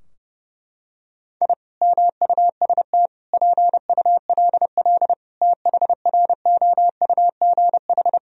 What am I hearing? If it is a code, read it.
Morse code, heard as a keyed tone, about 30 wpm, 700 Hz. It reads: I MUST PULL THROUGH
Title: Sentences from Top 1000 Words - Review Part 2 - 30wpm